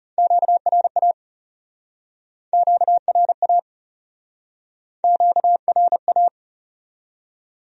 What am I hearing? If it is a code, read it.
Morse code, heard as a keyed tone, about 34 wpm, 700 Hz. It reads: QRA QRA QRA
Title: Speed-Racing Q-Codes - 30wpm